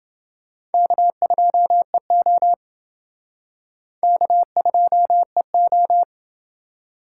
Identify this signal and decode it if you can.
Morse code, heard as a keyed tone, about 28 wpm, 700 Hz. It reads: K2EO K2EO